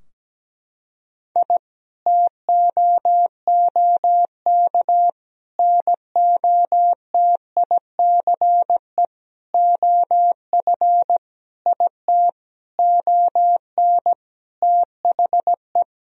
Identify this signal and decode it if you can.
Morse code, heard as a keyed tone, about 17 wpm, 700 Hz. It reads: I TOOK NOTICE OF IT ON THE
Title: Sentences from Top 1000 Words - Review Part 2 - 17wpm